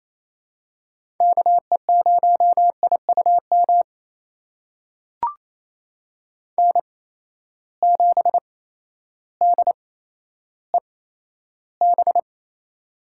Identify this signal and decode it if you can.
Morse code, heard as a keyed tone, about 28 wpm, spaced as for 13 wpm, 700 Hz. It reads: KE0IUM N7DEB
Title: USA Call Signs - 8x Character Spacing - 28wpm